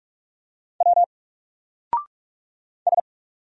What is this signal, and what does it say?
Morse code, heard as a keyed tone, about 44 wpm, 700 Hz. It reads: W S
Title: Single Character (Letter:Number:Punctuation:Prosign) 45wpm